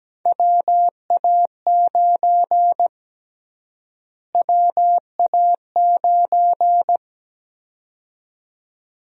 Morse code, heard as a keyed tone, about 17 wpm, 700 Hz. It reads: WA9 WA9
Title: USA Call Sign Prefixes - Repeated 2x - 17wpm